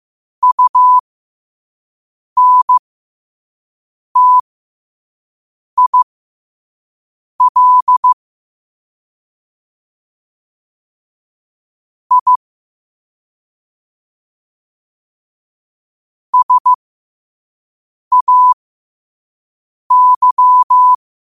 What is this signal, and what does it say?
Morse code, heard as a keyed tone, about 15 wpm, spaced as for 5 wpm, 1000 Hz. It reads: UNTIL I SAY
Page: Quotes for Fri, 15 Aug 2025 in Morse Code at 5 words per minute.